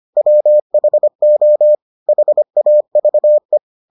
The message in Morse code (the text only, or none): WHO HAVE